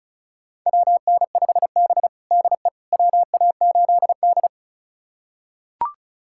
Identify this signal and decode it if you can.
Morse code, heard as a keyed tone, about 34 wpm, 700 Hz. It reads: WN5B DE WA8D
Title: USA Callsign DE Callsign - 4 Characters - 35wpm